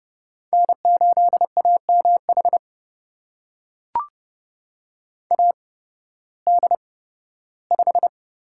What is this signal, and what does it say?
Morse code, heard as a keyed tone, about 31 wpm, spaced as for 16 wpm, 700 Hz. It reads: N8AMH AD5
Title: USA Call Signs - 5 Characters - 8x Character Spacing - 30wpm